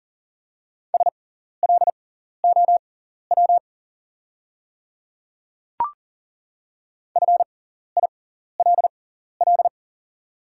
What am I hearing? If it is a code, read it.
Morse code, heard as a keyed tone, about 40 wpm, spaced as for 14 wpm, 700 Hz. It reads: SLOW FILL
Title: 6x Character Spacing - Top 500 Words - 40wpm